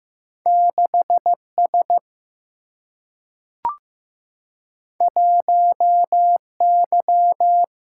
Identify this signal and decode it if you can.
Morse code, heard as a keyed tone, about 15 wpm, 700 Hz. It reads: 6 S 1 Y
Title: Single-Letter Single-Number 15wpm